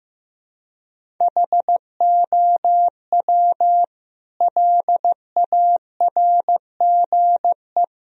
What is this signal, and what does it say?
Morse code, heard as a keyed tone, about 16 wpm, 700 Hz. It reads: HOW LARGE
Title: Sentences from Top 1000 Words - Review Part 1 - 15wpm